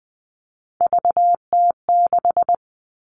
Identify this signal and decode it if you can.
Morse code, heard as a keyed tone, about 20 wpm, 700 Hz. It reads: VT6
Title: All Cut Numbers - 3-Digit 20wpm